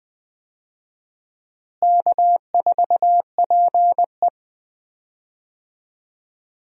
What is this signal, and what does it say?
Morse code, heard as a keyed tone, about 20 wpm, 700 Hz. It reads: K4PE
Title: USA Call Signs - 4 Characters - Repeated 3x - 20wpm